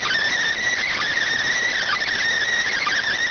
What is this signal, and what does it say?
Emergency Alert System (SAME) tones, heard as harsh buzzing bursts. It is heard from 0.6 to 3.1 s.